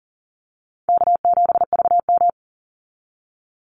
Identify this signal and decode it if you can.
Morse code, heard as a keyed tone, about 40 wpm, 700 Hz. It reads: K7VM
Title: Call Signs - 40wpm